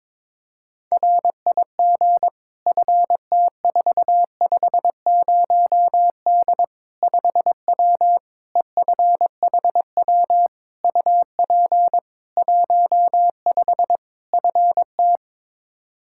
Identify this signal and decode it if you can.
Morse code, heard as a keyed tone, about 22 wpm, 700 Hz. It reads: RIG FT450D 5W EFHW UP 15 FT
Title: Practice QSO - Exchange 2 - Info - 22wpm